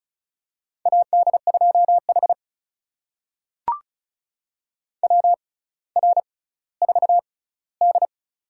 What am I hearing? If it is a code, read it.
Morse code, heard as a keyed tone, about 36 wpm, spaced as for 18 wpm, 700 Hz. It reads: AD2H WR4D
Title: USA Call Signs - 4 Characters - 6x Character Spacing - 35wpm